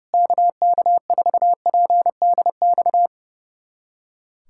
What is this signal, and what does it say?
Morse code, heard as a keyed tone, about 30 wpm, 700 Hz. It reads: KK4PDX
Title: USA Callsign DE Callsign - 6 Characters - 30wpm